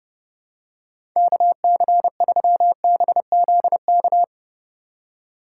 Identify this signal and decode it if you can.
Morse code, heard as a keyed tone, about 31 wpm, 700 Hz. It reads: KC3BZK